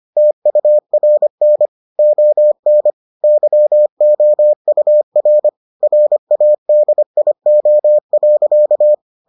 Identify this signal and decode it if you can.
Morse code, heard as a keyed tone, about 25 wpm, 600 Hz. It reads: TURN ON YOUR RADIO.